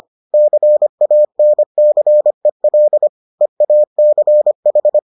Morse code, heard as a keyed tone, about 25 wpm, 600 Hz. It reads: CANCEL EACH